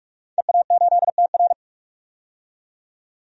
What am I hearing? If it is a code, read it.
Morse code, heard as a keyed tone, about 44 wpm, 700 Hz. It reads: EA8TR
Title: DXCC-100 Call Signs - 5 Characters - 45wpm